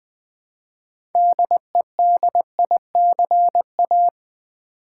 Morse code, heard as a keyed tone, about 20 wpm, 700 Hz. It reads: DEDICA
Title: Build A Word - 3 Letters At A Time - 9-Letter Words - 20wpm